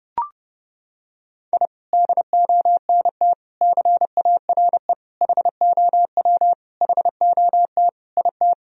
Morse code, heard as a keyed tone, about 32 wpm, 700 Hz. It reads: I DONT CARE HOW HOT IT
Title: Sentences from Top 1000 Words - Review Part 3 - 30wpm